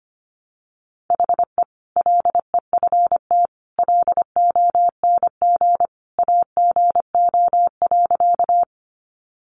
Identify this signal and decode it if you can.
Morse code, heard as a keyed tone, about 25 wpm, 700 Hz. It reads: HE LEFT LONG AGO.
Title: Sentences from Top 500 Words 25wpm